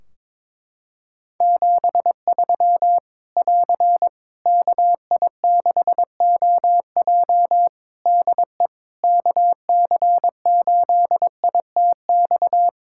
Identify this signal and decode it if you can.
Morse code, heard as a keyed tone, about 22 wpm, 700 Hz. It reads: 73 <AR> KI6OJ DE KC8ITX
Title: K1USN SST QSO Practice 22wpm